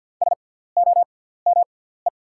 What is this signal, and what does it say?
Morse code, heard as a keyed tone, about 50 wpm, spaced as for 17 wpm, 700 Hz. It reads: SOME
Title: 6x Character Spacing - Top 100 Words - 50wpm